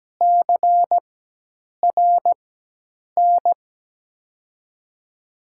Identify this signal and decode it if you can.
Morse code, heard as a keyed tone, about 17 wpm, spaced as for 8 wpm, 700 Hz. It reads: CRN
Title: USA Call Signs - 6 Characters - 4x Character Spacing - 17wpm